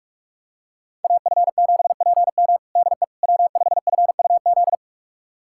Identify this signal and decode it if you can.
Morse code, heard as a keyed tone, about 46 wpm, 700 Hz. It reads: AF7PM DE W5FUZ
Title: USA Callsign DE Callsign - 5 Characters - 45wpm